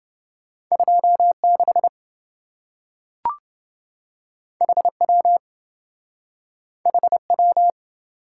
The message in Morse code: 26 HW HW

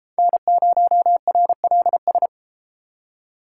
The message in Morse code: N0RLS

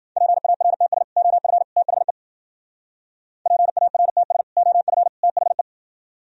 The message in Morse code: PARTS OF THE PARTS OF THE

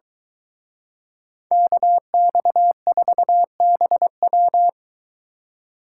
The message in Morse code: KX4BW

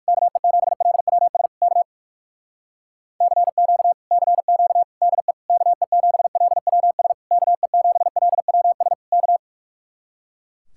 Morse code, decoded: KE7LWS K CQ CQ DE KE7LWS KE7LWS K

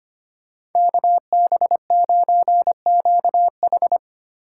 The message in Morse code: KB9QH

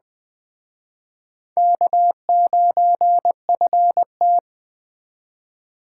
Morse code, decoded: K9FT

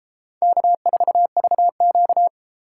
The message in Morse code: K4VQ

